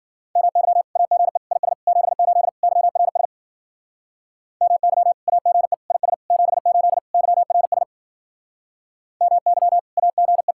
Morse code, heard as a keyed tone, about 49 wpm, 700 Hz. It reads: MY AGE IS 78 YRS MY AGE IS 78 YRS MY AGE
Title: Speed-Racing Common QSO Phrases - 40wpm